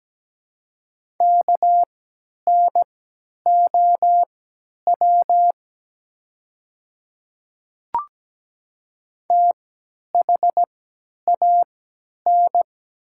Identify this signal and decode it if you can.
Morse code, heard as a keyed tone, about 17 wpm, spaced as for 10 wpm, 700 Hz. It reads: KNOW THAN
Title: 3x Character Spacing - Top 100 Words - 17wpm